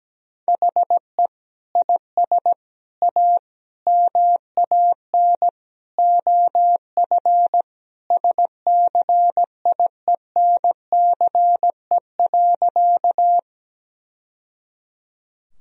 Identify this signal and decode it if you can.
Morse code, heard as a keyed tone, about 17 wpm, 700 Hz. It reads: HE IS A MAN OF SCIENCE.
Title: Sentences from Top 1000 Words - Review Part 2 - 17wpm